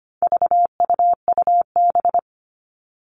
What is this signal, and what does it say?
Morse code, heard as a keyed tone, about 25 wpm, 700 Hz. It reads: VUUB
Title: All Cut Numbers - 4-Digit 25wpm